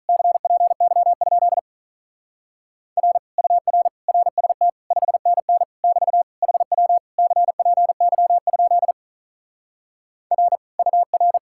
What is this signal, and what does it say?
Morse code, heard as a keyed tone, about 42 wpm, 700 Hz. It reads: CPY? R UR RST 5NN <BT> HW CPY? R UR
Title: Speed-Racing Common QSO Phrases - 35wpm